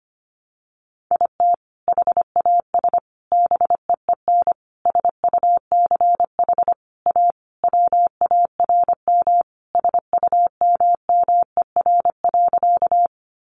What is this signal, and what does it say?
Morse code, heard as a keyed tone, about 25 wpm, 700 Hz. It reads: IT HAS BEEN SUCH A WARM SUMMER.
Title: Sentences from Top 1100 Words 25wpm